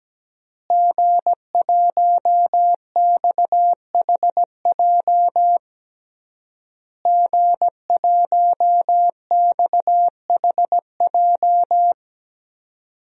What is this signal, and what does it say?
Morse code, heard as a keyed tone, about 17 wpm, 700 Hz. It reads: G1XHJ G1XHJ
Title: DXCC-100 Call Signs - 5 Characters - Repeated 3x - 17wpm